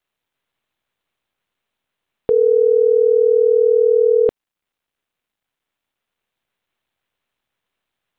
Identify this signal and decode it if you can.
Ringback tone (North America), 440 + 480 Hz.